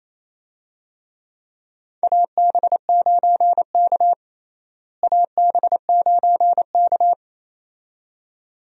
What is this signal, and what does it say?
Morse code, heard as a keyed tone, about 28 wpm, 700 Hz. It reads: AB9K AB9K
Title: USA Call Signs - 4 Characters - Repeated 2x - 28wpm